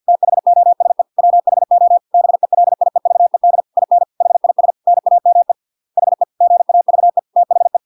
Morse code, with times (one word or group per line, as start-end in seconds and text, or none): 0.1-1.0 THOSE
1.2-2.0 WHO
2.1-3.6 BELIEVED
3.8-4.0 IN
4.2-4.7 HIS
4.9-5.5 NAME
6.0-6.2 HE
6.4-7.2 GAVE
7.4-7.8 THE